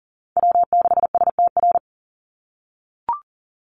Morse code, heard as a keyed tone, about 37 wpm, 700 Hz. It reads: W6STR
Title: NA SOTA Chaser Callsigns 40wpm